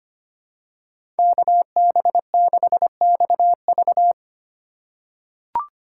Morse code, heard as a keyed tone, about 25 wpm, 700 Hz. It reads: KB6XV
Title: USA Call Signs - 2x Character Spacing - 25wpm